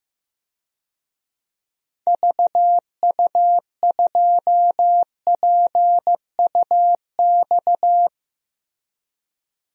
Morse code, heard as a keyed tone, about 15 wpm, 700 Hz. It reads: VU2PUX
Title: DXCC-100 Call Signs - 6 Characters - Repeated 3x - 15wpm